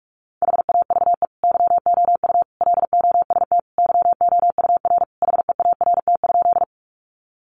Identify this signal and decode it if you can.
Morse code, heard as a keyed tone, about 44 wpm, 700 Hz. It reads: HAVE YOU LOST YOUR HEART?
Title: Sentences from Top 1400 Words 45wpm